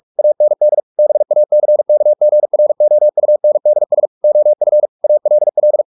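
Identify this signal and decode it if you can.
Morse code, heard as a keyed tone, about 44 wpm, 600 Hz. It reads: AND BACKGROUNDS OF ALL
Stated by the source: Morse practice at 45 wpm, the novel Never by Ken Follett in English, from CW med Gnister.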